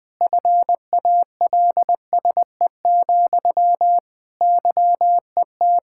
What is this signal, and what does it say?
Morse code, heard as a keyed tone, about 20 wpm, 700 Hz. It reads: FALSE, YET
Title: Sentences from Top 1600 Words 20wpm